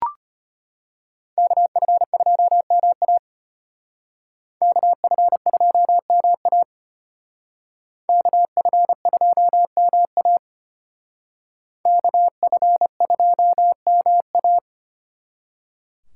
Morse code, heard as a keyed tone, about 31 wpm, 700 Hz. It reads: KF2MA KF2MA KF2MA KF2MA
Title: Speed Racing - USA 2x2 Call Signs 25wpm